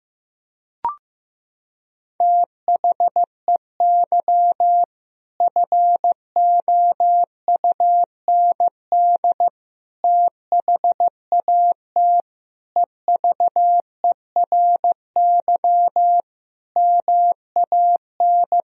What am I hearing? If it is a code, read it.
Morse code, heard as a keyed tone, about 15 wpm, 700 Hz. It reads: THEY FOUND THAT EVERY MAN
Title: Sentences from Top 1000 Words - Review Part 4 - 15wpm